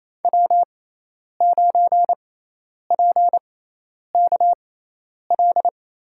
Morse code, heard as a keyed tone, about 28 wpm, spaced as for 10 wpm, 700 Hz. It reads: W9PKL
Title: USA Call Signs - 6x Character Spacing - 28wpm